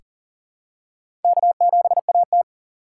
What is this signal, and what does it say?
Morse code, heard as a keyed tone, about 40 wpm, 700 Hz. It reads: K7AT
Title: USA Call Signs - 4 Characters - 2x Character Spacing - 40wpm